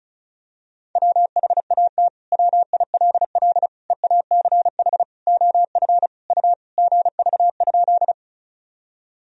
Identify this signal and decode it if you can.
Morse code, heard as a keyed tone, about 34 wpm, 700 Hz. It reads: WHAT WILL EACH OF U GV?
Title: Sentences from Top 200 Words and Abbreviations 35wpm